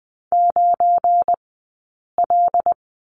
Morse code, heard as a keyed tone, about 20 wpm, 700 Hz. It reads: 9 L
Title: 177 Lesson 29d - Slash - Two Characters 20wpm